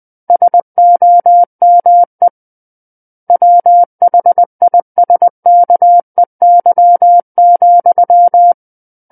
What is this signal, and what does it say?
Morse code, heard as a keyed tone, about 20 wpm, 700 Hz. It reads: SOME WHISKEY,
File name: 20 WPM